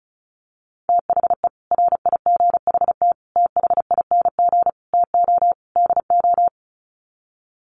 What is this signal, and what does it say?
Morse code, heard as a keyed tone, about 34 wpm, 700 Hz. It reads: THE RIGHT THING TO DO
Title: Sets of 5 Words 35wpm